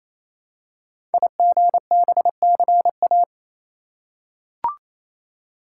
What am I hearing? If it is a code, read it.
Morse code, heard as a keyed tone, about 29 wpm, 700 Hz. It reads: IGBCA
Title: Code Groups - 5 Letters - 28wpm